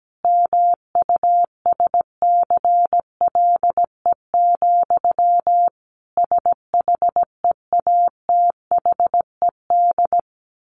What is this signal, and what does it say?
Morse code, heard as a keyed tone, about 17 wpm, 700 Hz. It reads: MUSCLE, SHEATHED
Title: A Sound Of Thunder 17wpm